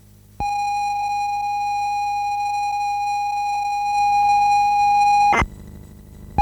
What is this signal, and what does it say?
ZX Spectrum tape loading.
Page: Как звучали игры ZX Spectrum